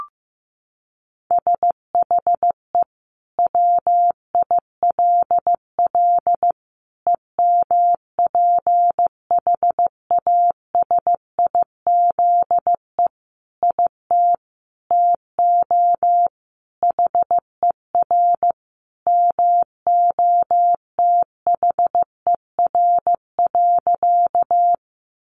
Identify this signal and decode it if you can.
Morse code, heard as a keyed tone, about 15 wpm, 700 Hz. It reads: SHE WILL EMPHASIZE IT TO HER MOTHER.
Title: Sentences from Top 1500 Words 15wpm